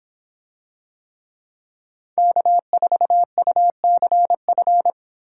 Morse code, heard as a keyed tone, about 26 wpm, 700 Hz. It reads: K4UCF